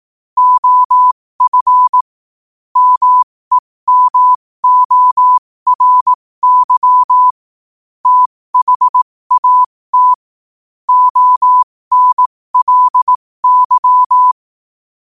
Morse code, heard as a keyed tone, about 18 wpm, spaced as for 15 wpm, 1000 Hz. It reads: OF MEMORY THAT ONLY
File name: qotdfiles / 15wpm